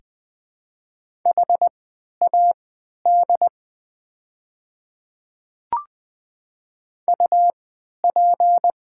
Morse code, heard as a keyed tone, about 20 wpm, spaced as for 11 wpm, 700 Hz. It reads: HAD UP
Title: 3x Character Spacing - Top 300 Words - 20wpm